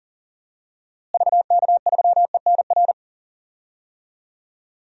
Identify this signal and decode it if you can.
Morse code, heard as a keyed tone, about 40 wpm, 700 Hz. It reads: VK3ENR
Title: DXCC-100 Call Signs - 6 Characters - Repeated 3x - 40wpm